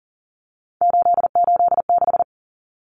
Morse code, heard as a keyed tone, about 40 wpm, 700 Hz. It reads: 886
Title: Numbers - 3-Digit 40wpm